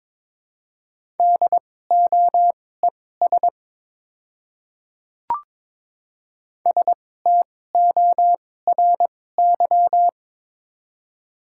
Morse code, heard as a keyed tone, about 22 wpm, spaced as for 16 wpm, 700 Hz. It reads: DOES STORY
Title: 2x Character Spacing - Top 300 Words - 22wpm